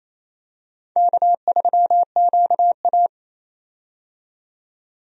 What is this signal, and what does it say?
Morse code, heard as a keyed tone, about 28 wpm, 690 Hz. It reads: K3QA
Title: USA Call Signs - 4 Characters - 28wpm